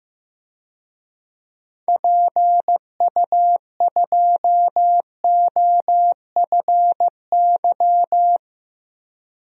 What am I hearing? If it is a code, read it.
Morse code, heard as a keyed tone, about 15 wpm, 690 Hz. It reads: PU2OFY
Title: DXCC-100 Call Signs - 6 Characters - 15wpm